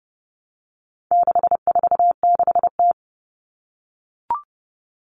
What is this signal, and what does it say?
Morse code, heard as a keyed tone, about 30 wpm, 700 Hz. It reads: 646T
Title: All Cut Numbers - 4-Digit 30wpm